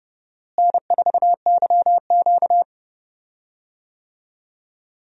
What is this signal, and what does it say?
Morse code, heard as a keyed tone, about 30 wpm, 700 Hz. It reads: N4YQ
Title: USA Call Signs - 4 Characters - Repeated 2x - 30wpm